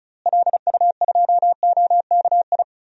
Morse code, heard as a keyed tone, about 36 wpm, 700 Hz. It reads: LU2OKI
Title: DXCC-100 Call Signs - 6 Characters - Repeated 3x - 35wpm